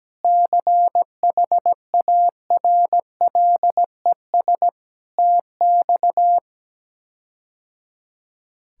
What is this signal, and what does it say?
Morse code, heard as a keyed tone, about 17 wpm, 700 Hz. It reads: CHARLES TX
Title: K1USN SST QSO Practice - 17wpm